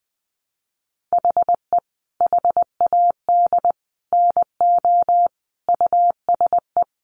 Morse code, heard as a keyed tone, about 20 wpm, 700 Hz. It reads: HE HAD NO USE
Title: Sentences from Top 1200 Words 20wpm